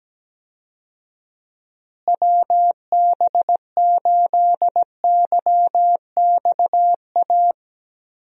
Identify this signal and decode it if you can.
Morse code, heard as a keyed tone, about 17 wpm, 700 Hz. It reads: WB8YXA
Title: USA Call Signs - 6 Characters - Repeated 2x - 17wpm